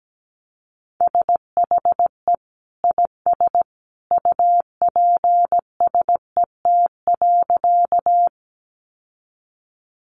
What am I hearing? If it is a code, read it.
Morse code, heard as a keyed tone, about 17 wpm, 700 Hz. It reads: SHE IS UPSET.
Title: Sentences from Top 800 Words 17wpm